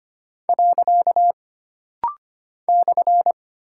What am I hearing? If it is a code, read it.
Morse code, heard as a keyed tone, about 25 wpm, 700 Hz. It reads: . /
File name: Single Punctuation-Prosign - Rapid-Fire - 25wpm